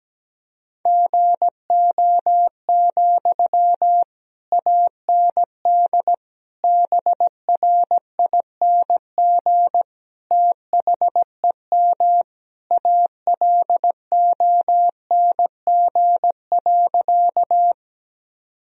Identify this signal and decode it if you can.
Morse code, heard as a keyed tone, about 17 wpm, 700 Hz. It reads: GO, AND BRING THEM ALONG.
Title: Sentences from Top 1000 Words - Review Part 3 - 17wpm